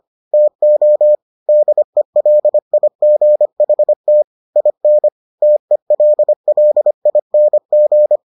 Morse code, heard as a keyed tone, about 25 wpm, 600 Hz. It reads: TO DELIGHT IN TELLING